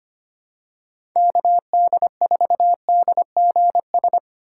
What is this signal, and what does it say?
Morse code, heard as a keyed tone, about 25 wpm, 700 Hz. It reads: KD4DGS
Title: USA Call Signs - 6 Characters - 4x Character Spacing - 25wpm